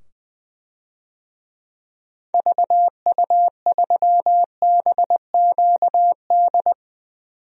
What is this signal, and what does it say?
Morse code, heard as a keyed tone, about 21 wpm, 700 Hz. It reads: VU3BQD
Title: DXCC-100 Call Signs - 6 Characters - 20wpm